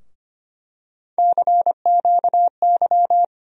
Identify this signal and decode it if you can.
Morse code, heard as a keyed tone, about 25 wpm, 700 Hz. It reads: CQY
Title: USA Call Sign Postfixes - 3 Letters 25wpm